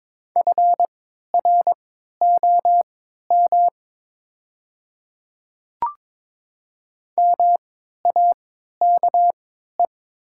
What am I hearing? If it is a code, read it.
Morse code, heard as a keyed tone, about 23 wpm, spaced as for 13 wpm, 700 Hz. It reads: FROM MAKE
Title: 3x Character Spacing - Top 100 Words - 22wpm